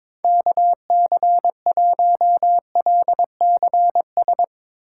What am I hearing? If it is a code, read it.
Morse code, heard as a keyed tone, about 22 wpm, 700 Hz. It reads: KC1LCS
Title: USA Call Signs - 6 Characters - 8x Character Spacing - 22wpm